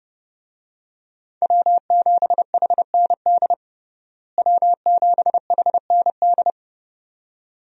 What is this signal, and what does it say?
Morse code, heard as a keyed tone, about 30 wpm, 700 Hz. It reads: W7HND W7HND
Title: USA Call Signs - 5 Characters - Repeated 2x - 30wpm